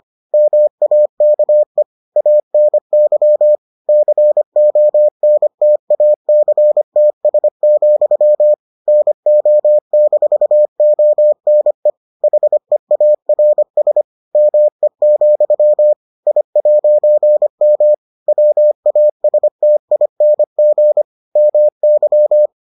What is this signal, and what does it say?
Morse code, heard as a keyed tone, about 25 wpm, 600 Hz. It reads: MAKE ANY CONTACTS, NO-ONE HEARS ME, I'M WASTING MY